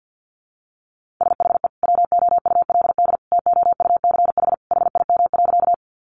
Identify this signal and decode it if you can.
Morse code, heard as a keyed tone, about 50 wpm, 700 Hz. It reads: SHE WOULD TOUCH HIM.
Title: Sentences from Top 1200 Words 50wpm